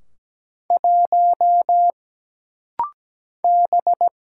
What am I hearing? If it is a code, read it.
Morse code, heard as a keyed tone, about 17 wpm, 700 Hz. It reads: 1 B
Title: Single Letter-Number - Rapid-Fire 17wpm